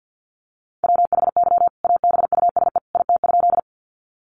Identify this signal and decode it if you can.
Morse code, heard as a keyed tone, about 50 wpm, 700 Hz. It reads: WHY ABUSE IT?
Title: Sentences from Top 1300 Words 50wpm